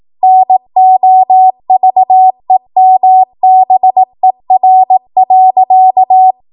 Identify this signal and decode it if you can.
Morse code, heard as a keyed tone, about 18 wpm, 750 Hz. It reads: NOVEMBER.
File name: Archive / 18 WPM / 18WPM